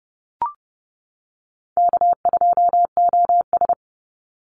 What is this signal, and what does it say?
Morse code, heard as a keyed tone, about 30 wpm, 700 Hz. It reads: K2OS
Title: Call Signs - 30wpm